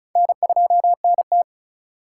Morse code, heard as a keyed tone, about 36 wpm, 700 Hz. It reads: N2NT
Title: USA Call Signs - 4 Characters - 2x Character Spacing - 35wpm